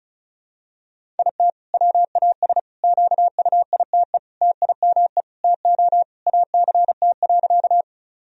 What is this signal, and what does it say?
Morse code, heard as a keyed tone, about 34 wpm, 700 Hz. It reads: IT WAS QUITE TIME TO ACT.
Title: Sentences from Top 1000 Words - Review Part 1 - 35wpm